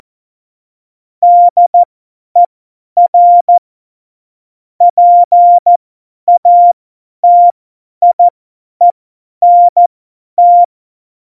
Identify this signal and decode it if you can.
Morse code, heard as a keyed tone, about 14 wpm, spaced as for 10 wpm, 700 Hz. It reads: DER PATIENT
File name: Morsekurs / Witze / Witz 22. September 2020, 50 F70